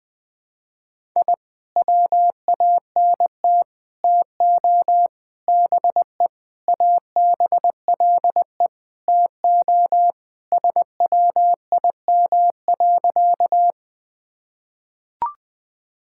Morse code, heard as a keyed tone, about 20 wpm, 700 Hz. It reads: I WANT TO BE ABLE TO SWIM.
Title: Sentences with 5-Word Phrases - 20wpm